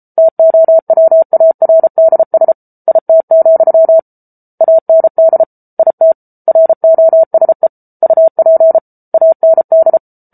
Morse code, heard as a keyed tone, about 34 wpm, 650 Hz. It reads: TOWARDS IT, AND IT ROSE UP AND